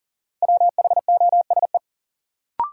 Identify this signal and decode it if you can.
Morse code, heard as a keyed tone, about 40 wpm, 700 Hz. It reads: WHOSE
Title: Sentences from Top 1200 Words - New Word Drill - 40wpm